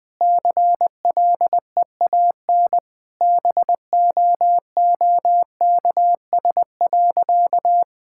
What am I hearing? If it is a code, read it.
Morse code, heard as a keyed tone, about 21 wpm, 700 Hz. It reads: CLEAN BOOKS.
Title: Sentences with 4-Word Phrases - Spoken Beginning - 20wpm